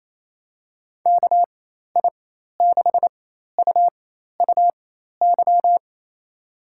Morse code, heard as a keyed tone, about 28 wpm, spaced as for 13 wpm, 700 Hz. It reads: KI6UUY
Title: USA Call Signs - 6 Characters - 4x Character Spacing - 28wpm